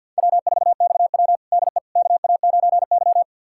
Morse code, heard as a keyed tone, about 51 wpm, 700 Hz. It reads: W4XW DE KA9Y
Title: USA Callsign DE Callsign - 4 Characters - 50wpm